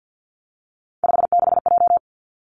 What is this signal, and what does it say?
Morse code, heard as a keyed tone, about 50 wpm, 700 Hz. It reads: 56J
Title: Three Challenging Letter-Number Combinations 50wpm